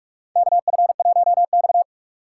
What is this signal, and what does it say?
Morse code, heard as a keyed tone, about 47 wpm, 700 Hz. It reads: KF1X
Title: USA Call Signs - 4 Characters - 6x Character Spacing - 45wpm